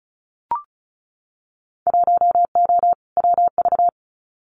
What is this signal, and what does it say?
Morse code, heard as a keyed tone, about 36 wpm, 700 Hz. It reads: 1O WV
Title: Winter Field Day Exchanges Encore 35wpm